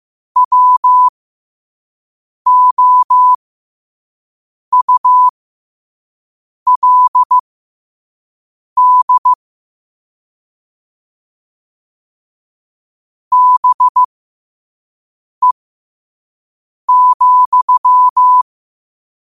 Morse code, heard as a keyed tone, about 15 wpm, spaced as for 5 wpm, 1000 Hz. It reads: WOULD BE,
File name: qotdfiles / 5wpm